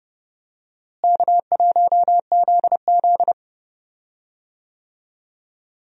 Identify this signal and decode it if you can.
Morse code, heard as a keyed tone, about 32 wpm, 700 Hz. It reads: K1ZZ
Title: USA Call Signs - 4 Characters - Repeated 3x - 30wpm